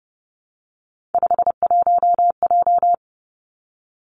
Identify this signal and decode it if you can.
Morse code, heard as a keyed tone, about 31 wpm, 700 Hz. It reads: 51J
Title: Three Challenging Letter-Number Combinations 30wpm